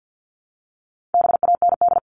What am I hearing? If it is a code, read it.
Morse code, heard as a keyed tone, about 50 wpm, 700 Hz. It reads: BAND
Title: Top 1501-1750 Words 50wpm